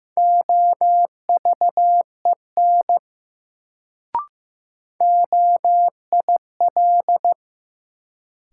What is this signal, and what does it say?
Morse code, heard as a keyed tone, about 15 wpm, 700 Hz. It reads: OVEN OIL
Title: Sentences from Top 1000 Words - Review Part 4 - New Word Drill - 15wpm